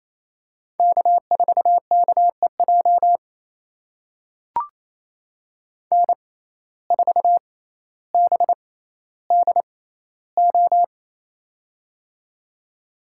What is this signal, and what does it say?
Morse code, heard as a keyed tone, about 28 wpm, spaced as for 14 wpm, 700 Hz. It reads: K4KEJ N4BDO
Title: USA Call Signs - 5 Characters - 6x Character Spacing - 28wpm